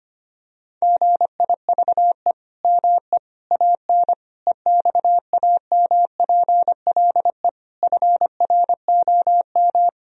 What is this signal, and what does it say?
Morse code, heard as a keyed tone, about 25 wpm, 700 Hz. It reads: GIVE ME AN EXAMPLE FROM
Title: Sentences from Top 1000 Words - Review Part 3 - 25wpm